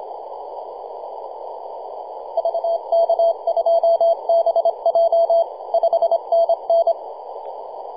Morse code, heard as a keyed tone, about 26 wpm, 670 Hz. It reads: VK2BJ 5NN
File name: posts / 2015-sept-26-2040z-v73a-21-025-cw